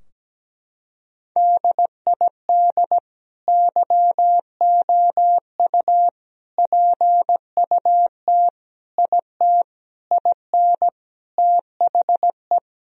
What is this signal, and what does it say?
Morse code, heard as a keyed tone, about 17 wpm, 700 Hz. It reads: DID YOU PUT IT IN THE
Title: Sentences from Top 1000 Words - Review Part 4 - 17wpm